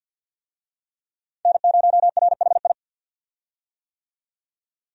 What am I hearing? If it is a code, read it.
Morse code, heard as a keyed tone, about 49 wpm, 700 Hz. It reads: N0RHI
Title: USA Call Signs - 5 Characters - 50wpm